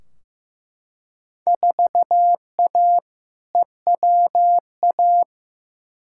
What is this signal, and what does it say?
Morse code, heard as a keyed tone, about 15 wpm, 700 Hz. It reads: 4A EWA
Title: Field Day Exchanges Encore - 15wpm